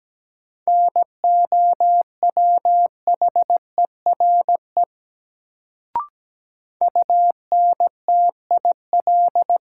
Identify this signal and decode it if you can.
Morse code, heard as a keyed tone, about 17 wpm, 700 Hz. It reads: NOWHERE UNTIL
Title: Sets of 5 Words Encore - New Word Drill - 17wpm